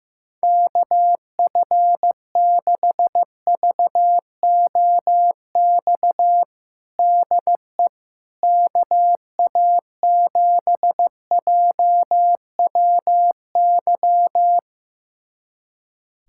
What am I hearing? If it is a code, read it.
Morse code, heard as a keyed tone, about 15 wpm, 700 Hz. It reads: KF6VOX DE KA7JWY
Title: USA Callsign DE Callsign - 6 Characters - 15wpm